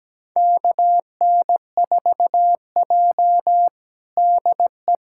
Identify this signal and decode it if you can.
Morse code, heard as a keyed tone, about 17 wpm, 700 Hz. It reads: KN4J DE
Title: USA Callsign DE Callsign - 4 Characters - 17wpm